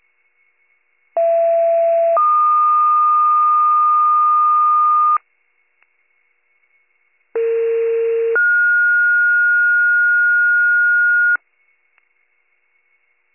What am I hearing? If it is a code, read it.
Two-tone page, heard as a one-second tone, then a three-second tone, about 670 then 1220 Hz, about 460 then 1430 Hz. It is heard from 1.2 to 11.4 s.